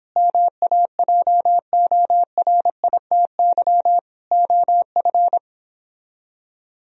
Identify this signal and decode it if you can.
Morse code, heard as a keyed tone, about 31 wpm, 700 Hz. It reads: MAJORITY OF